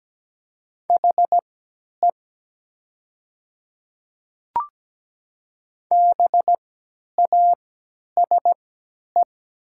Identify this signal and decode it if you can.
Morse code, heard as a keyed tone, about 17 wpm, spaced as for 10 wpm, 700 Hz. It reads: HE BASE
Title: 3x Character Spacing - Top 300 Words - 17wpm